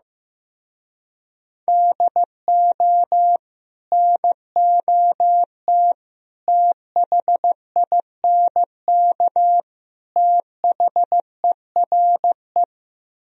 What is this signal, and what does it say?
Morse code, heard as a keyed tone, about 16 wpm, 700 Hz. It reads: DO NOT THINK THERE
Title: Speed-Racing - Sets of 4 Words 15wpm